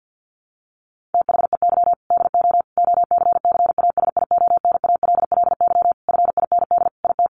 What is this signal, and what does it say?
Morse code, heard as a keyed tone, about 50 wpm, 700 Hz. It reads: THEY DO OCCASIONALLY FIND IT
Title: Sentences from Top 1300 Words 50wpm